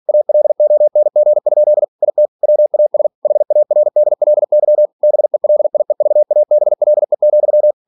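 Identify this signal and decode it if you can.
Morse code, heard as a keyed tone, about 46 wpm, 600 Hz. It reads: ALONG? IT WAS HARDLY BELIEVABLE,
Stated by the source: Practice text from an English novel sent at 47 WPM.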